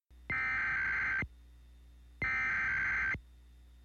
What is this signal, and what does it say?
Emergency Alert System (SAME) tones (header bursts), heard as harsh buzzing bursts. It is heard from 0.3 to 3.1 s.
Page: EAS Tone NWS Alert